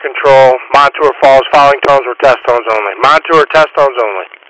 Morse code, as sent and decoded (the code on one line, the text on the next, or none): none